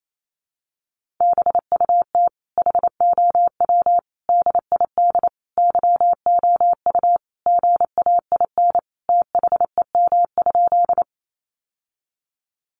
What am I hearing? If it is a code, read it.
Morse code, heard as a keyed tone, about 28 wpm, 700 Hz. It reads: BUT HOW DID YOU GAIN THEM?
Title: Sentences from Top 1000 Words 28wpm